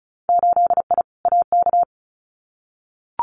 Morse code, heard as a keyed tone, about 36 wpm, 700 Hz. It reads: 8I AK
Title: Winter Field Day Exchanges Encore 35wpm